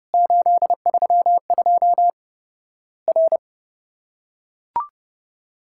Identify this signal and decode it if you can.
Morse code, heard as a keyed tone, about 31 wpm, 700 Hz. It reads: 832 R
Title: Worked All Europe DX Contest - QTC Exchange - 30wpm